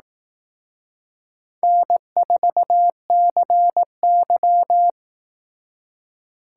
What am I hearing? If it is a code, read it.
Morse code, heard as a keyed tone, about 18 wpm, 700 Hz. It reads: N4CY
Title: Speed Racing - USA 1x2 Call Signs 15wpm